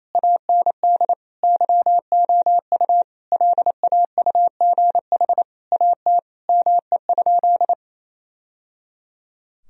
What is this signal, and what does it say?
Morse code, heard as a keyed tone, about 28 wpm, 700 Hz. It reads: AND YOU LAUGH AT ME?
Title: Sentences from Top 1000 Words - Review Part 2 - 28wpm